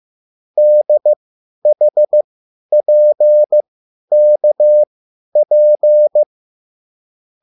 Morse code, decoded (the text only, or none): DHPKP